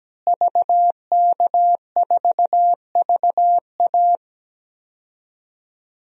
Morse code, heard as a keyed tone, about 17 wpm, 700 Hz. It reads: VK4VA
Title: DXCC-100 Call Signs - 5 Characters - Repeated 3x - 17wpm